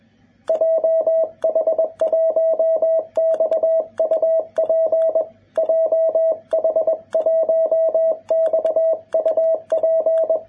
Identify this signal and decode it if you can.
Morse code, heard as a keyed tone, about 21 wpm, 650 Hz, with radio noise. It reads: JH1XUP JH1XUP
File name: jh8cbh / 2021cw